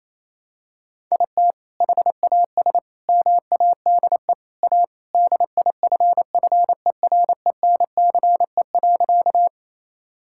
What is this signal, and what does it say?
Morse code, heard as a keyed tone, about 27 wpm, 700 Hz. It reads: IT HAS MADE A DIFFERENCE.
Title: Sentences from Top 1000 Words - Review Part 5 - 28wpm